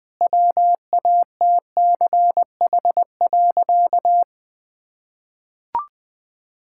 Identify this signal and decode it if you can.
Morse code, heard as a keyed tone, about 20 wpm, 700 Hz. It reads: WATCH.
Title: Sentences from Top 1600 Words 20wpm